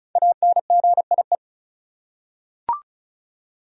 Morse code, heard as a keyed tone, about 36 wpm, 700 Hz. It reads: ANGIE
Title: USA Common Names - 35wpm